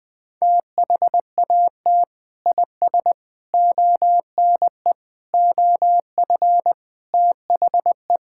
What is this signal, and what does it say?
Morse code, heard as a keyed tone, about 21 wpm, 700 Hz. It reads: THAT IS ONE OF THE
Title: Sentences with 5-Word Phrases - 20wpm